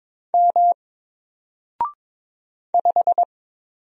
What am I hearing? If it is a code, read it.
Morse code, heard as a keyed tone, about 22 wpm, 700 Hz. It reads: M 5
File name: Single-Letter-Number-22wpm